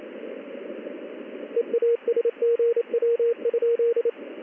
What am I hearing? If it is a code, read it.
Morse code, heard as a keyed tone, about 28 wpm, 470 Hz, with radio noise. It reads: EASGW?
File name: contests / ss-ssb-k5tr- -1700